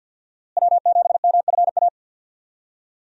Morse code, heard as a keyed tone, about 50 wpm, 700 Hz. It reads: W7MFA
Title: USA Call Signs - 5 Characters - Repeated 2x - 50wpm